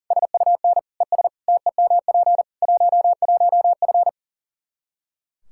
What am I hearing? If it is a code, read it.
Morse code, heard as a keyed tone, about 40 wpm, 700 Hz. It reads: SUN ES TEMP 11F
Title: Practice QSO - Exchange 2 - Info - 40wpm